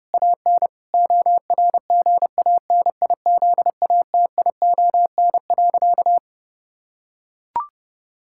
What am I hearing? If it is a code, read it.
Morse code, heard as a keyed tone, about 30 wpm, 700 Hz. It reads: AN ORGANIZATION.
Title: Sentences from Top 1000 Words - Review Part 5 - 30wpm